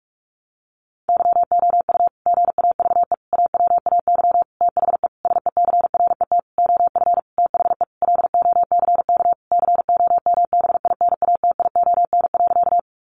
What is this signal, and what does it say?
Morse code, heard as a keyed tone, about 46 wpm, 700 Hz. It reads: YOU GAVE AWAY THE SECRET OF THE LOCK COMBINATION.
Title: Sentences from Top 1300 Words 45wpm